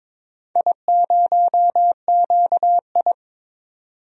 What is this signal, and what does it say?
Morse code, heard as a keyed tone, about 22 wpm, 700 Hz. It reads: I0QI
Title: DXCC-100 Call Signs - 4 Characters - 22wpm